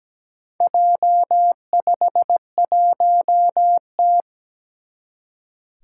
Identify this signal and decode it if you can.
Morse code, heard as a keyed tone, about 17 wpm, 700 Hz. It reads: J51T
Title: Four Challenging Letter-Number Combinations 17wpm